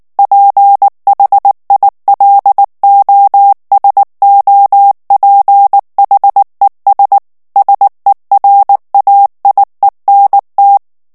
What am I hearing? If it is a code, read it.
Morse code, heard as a keyed tone, about 19 wpm, 800 Hz. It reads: PHILOSOPHES SERAIENT
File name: CW / FICHIERS-MP3 / FRANCAIS TEXTE CLAIR 2 VITESSE 18 MOTS MINUTES